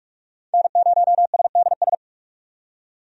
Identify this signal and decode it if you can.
Morse code, heard as a keyed tone, about 47 wpm, 700 Hz. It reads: N0SDS
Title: USA Call Signs - 5 Characters - 45wpm